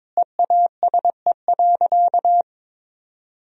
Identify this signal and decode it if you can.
Morse code, heard as a keyed tone, about 22 wpm, 700 Hz. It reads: EASE.
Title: Sentences from Top 1000 Words - Review Part 1 - 22wpm